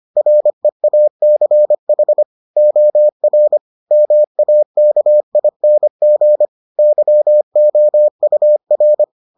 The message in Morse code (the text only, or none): REACH OR MAKING YOUR